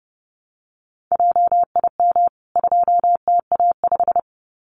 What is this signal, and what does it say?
Morse code, heard as a keyed tone, about 30 wpm, 700 Hz. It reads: JIM 2TA5
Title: CWT Exchange 30wpm